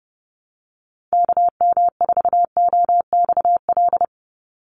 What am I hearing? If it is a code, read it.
Morse code, heard as a keyed tone, about 30 wpm, 700 Hz. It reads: KM4OXL
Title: USA Call Signs 30wpm